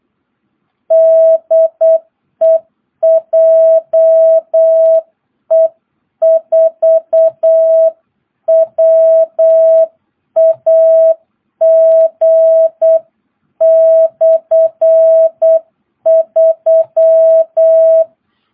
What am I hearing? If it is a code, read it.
Morse code, heard as a keyed tone, about 8 wpm, 660 Hz. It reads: DE JE4WAG/3
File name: jh8cbh / 2022cw